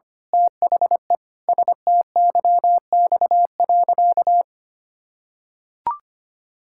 Morse code, heard as a keyed tone, about 25 wpm, 700 Hz. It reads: THE STYX.
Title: Pangram Challenge v2 - 25wpm